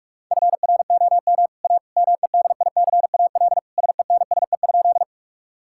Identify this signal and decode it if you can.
Morse code, heard as a keyed tone, about 47 wpm, 700 Hz. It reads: FROM A MEDICAL SENSE?
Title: Sentences from Top 1000 Words - Review Part 5 - 45wpm